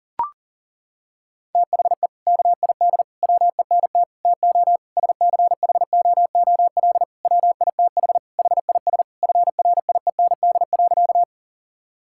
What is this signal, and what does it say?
Morse code, heard as a keyed tone, about 40 wpm, 700 Hz. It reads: THE KID WENT TO SCHOOL WITH HIS FRIEND.
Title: Sentences from Top 1600 Words 40wpm